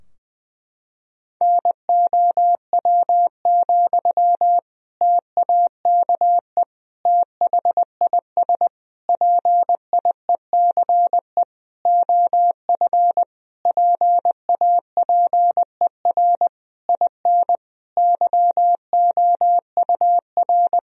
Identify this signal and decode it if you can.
Morse code, heard as a keyed tone, about 20 wpm, 700 Hz. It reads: NOW, TAKE THIS PIECE OF PAPER IN YOUR
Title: Sentences from Top 1000 Words - Review Part 2 - 20wpm